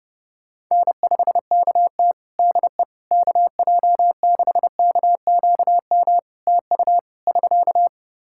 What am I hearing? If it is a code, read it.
Morse code, heard as a keyed tone, about 32 wpm, 700 Hz. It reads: N5KT DE KJ6KQM TU <SK>
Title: Practice QSO - Exchange 4 - Ending - 30wpm